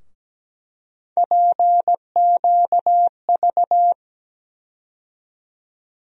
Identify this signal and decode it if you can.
Morse code, heard as a keyed tone, about 17 wpm, 700 Hz. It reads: PQV
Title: USA Call Sign Postfixes - 3 Letters 17wpm